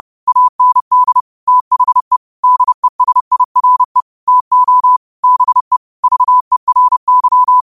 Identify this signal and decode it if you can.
Morse code, heard as a keyed tone, about 30 wpm, 1000 Hz. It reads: AND THE DESIRE TO BE VERY
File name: qotdfiles / 30wpm